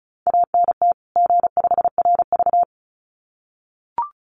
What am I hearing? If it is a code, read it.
Morse code, heard as a keyed tone, about 34 wpm, 700 Hz. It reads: ANT G5RV
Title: Common QSO Phrases 35wpm